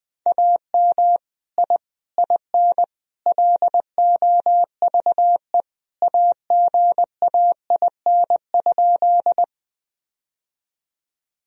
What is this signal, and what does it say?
Morse code, heard as a keyed tone, about 20 wpm, 700 Hz. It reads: AM I IN LOVE AGAIN?
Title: Sentences from Top 1000 Words - Review Part 2 - 20wpm